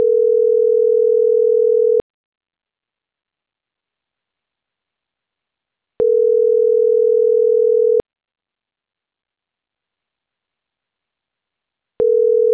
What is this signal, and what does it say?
Ringback tone (North America), 440 + 480 Hz.